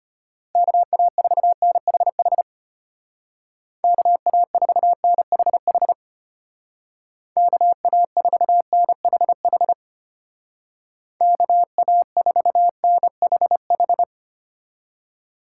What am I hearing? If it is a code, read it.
Morse code, heard as a keyed tone, about 33 wpm, 700 Hz. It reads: KA4NHH KA4NHH KA4NHH KA4NHH